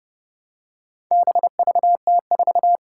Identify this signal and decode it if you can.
Morse code, heard as a keyed tone, about 30 wpm, 700 Hz. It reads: BVT4
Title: Four Challenging Letter-Number Combinations 30wpm